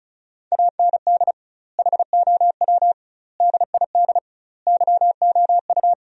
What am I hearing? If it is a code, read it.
Morse code, heard as a keyed tone, about 36 wpm, 700 Hz. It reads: AND HOW DID YOU
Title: 2x Word-Spacing - Sentences from Top 100 Words - 35wpm